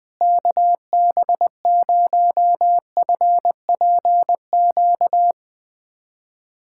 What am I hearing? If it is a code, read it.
Morse code, heard as a keyed tone, about 20 wpm, 700 Hz. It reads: KB0FPQ
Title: USA Callsign DE Callsign 20wpm